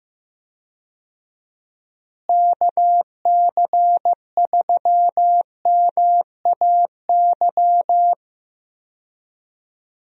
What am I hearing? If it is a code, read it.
Morse code, heard as a keyed tone, about 15 wpm, 700 Hz. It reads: KC3MAY
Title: USA Call Signs - 6 Characters - Repeated 3x - 15wpm